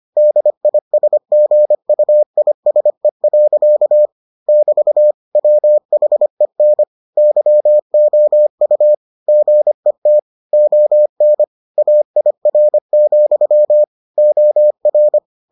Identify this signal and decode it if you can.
Morse code, heard as a keyed tone, about 25 wpm, 600 Hz. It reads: DISGUISE. <BT> WHEN YOU GET ON AIR, OR